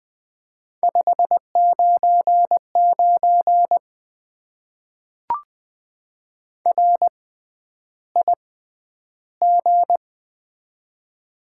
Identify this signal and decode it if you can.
Morse code, heard as a keyed tone, about 20 wpm, spaced as for 10 wpm, 700 Hz. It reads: 599 RIG
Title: Top 90 QSO Elements - 6x Character Spacing - 20wpm